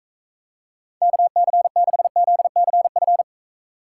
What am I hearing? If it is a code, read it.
Morse code, heard as a keyed tone, about 33 wpm, 700 Hz. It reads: KC6ZCF
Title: Speed Racing - USA 2x3 Call Signs 28wpm